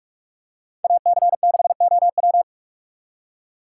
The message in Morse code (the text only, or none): AC6OW